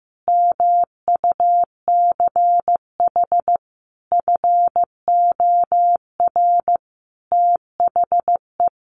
Morse code, heard as a keyed tone, about 15 wpm, 700 Hz. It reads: MUCH FOR THE
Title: Sets of 5 Words Encore2 15wpm